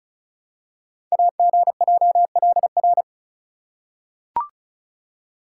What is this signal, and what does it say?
Morse code, heard as a keyed tone, about 36 wpm, 700 Hz. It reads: AGJLR
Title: Code Groups - 5 Letters - 35wpm